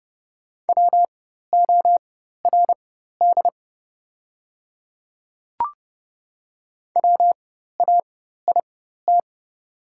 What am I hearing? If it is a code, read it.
Morse code, heard as a keyed tone, about 30 wpm, spaced as for 14 wpm, 690 Hz. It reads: WORD WAIT
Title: 4x Character Spacing - Top 500 Words - 30wpm